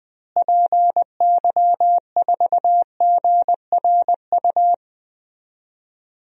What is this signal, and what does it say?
Morse code, heard as a keyed tone, about 20 wpm, 700 Hz. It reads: PY4GRU
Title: DXCC-100 Call Signs - 6 Characters - Repeated 2x - 20wpm